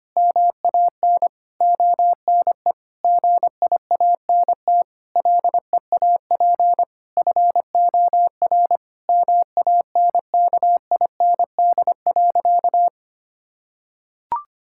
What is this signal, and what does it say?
Morse code, heard as a keyed tone, about 25 wpm, 700 Hz. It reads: MAN ONE GIANT LEAP FOR MANKIND.
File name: practice / 25wpm / Sentences with 9-Word Phrases - 25wpm